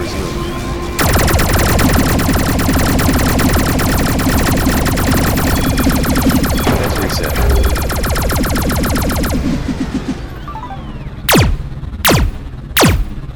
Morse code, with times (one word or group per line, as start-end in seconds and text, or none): none